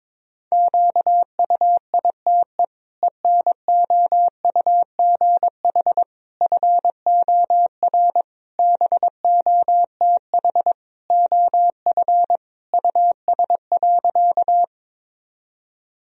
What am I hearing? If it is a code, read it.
Morse code, heard as a keyed tone, about 22 wpm, 700 Hz. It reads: QUITE ENOUGH FOR BOTH OF US.
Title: Sentences from Top 1000 Words - Review Part 1 - 22wpm